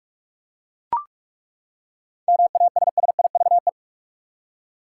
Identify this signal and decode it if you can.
Morse code, heard as a keyed tone, about 43 wpm, 700 Hz. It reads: MASSIVE
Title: Seven Letter Words - 45wpm